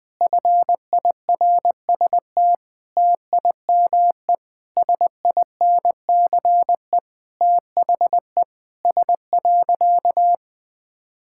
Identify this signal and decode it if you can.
Morse code, heard as a keyed tone, about 21 wpm, 700 Hz. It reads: FIRST TIME SINCE THE S.
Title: Sentences with 6-Word Phrases - Spoken Beginning - 20wpm